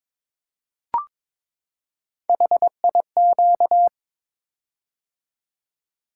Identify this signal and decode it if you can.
Morse code, heard as a keyed tone, about 22 wpm, 700 Hz. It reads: HIQ
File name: practice / 22wpm / Three Letter-Number Combinations - 22wpm